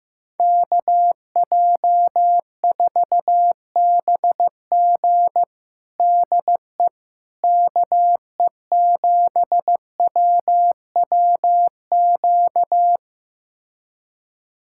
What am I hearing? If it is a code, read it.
Morse code, heard as a keyed tone, about 15 wpm, 700 Hz. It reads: KJ4BG DE KE7WWQ
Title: USA Callsign DE Callsign 15wpm